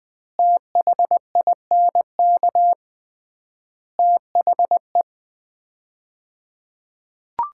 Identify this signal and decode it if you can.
Morse code, heard as a keyed tone, about 20 wpm, 700 Hz. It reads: THINK THE
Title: 3x Word-Spacing - Sets of 5 Words - 20wpm